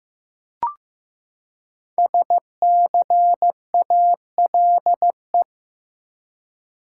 Morse code, heard as a keyed tone, about 15 wpm, 700 Hz. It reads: SCALE
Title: Five Letter Words - 15wpm